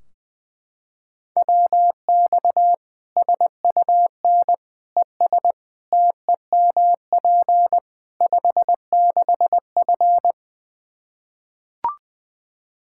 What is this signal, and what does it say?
Morse code, heard as a keyed tone, about 20 wpm, 700 Hz. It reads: WX SUN ES TEMP 56F
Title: Practice QSO - Exchange 2 - Info - 20wpm